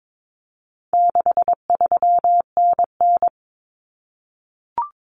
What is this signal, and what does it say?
Morse code, heard as a keyed tone, about 22 wpm, 700 Hz. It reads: 63NN
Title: Common Cut Numbers - 4-Digit 22wpm